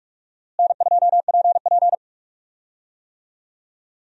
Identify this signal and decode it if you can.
Morse code, heard as a keyed tone, about 46 wpm, 700 Hz. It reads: N2PP
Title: USA Call Signs - 4 Characters - Repeated 2x - 45wpm